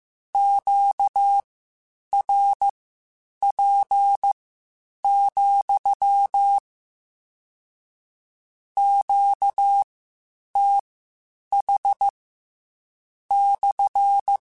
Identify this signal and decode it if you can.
Morse code, heard as a keyed tone, about 15 wpm, spaced as for 8 wpm, 800 Hz. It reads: QRP, QTH /